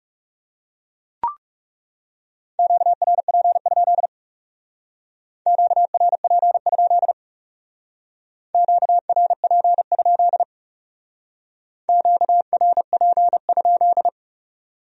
Morse code, heard as a keyed tone, about 40 wpm, 700 Hz. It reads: QRP? QRP? QRP? QRP?